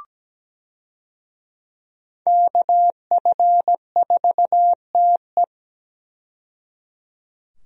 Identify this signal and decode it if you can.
Morse code, heard as a keyed tone, about 17 wpm, 700 Hz. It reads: KF4TE
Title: USA Call Signs - 5 Characters - 17wpm